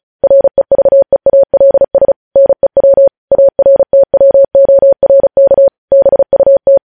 Morse code, heard as a keyed tone, about 36 wpm, 550 Hz. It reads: REVEALS NEW ARTWORK BUT
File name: podcast / News headlines-1x-35x35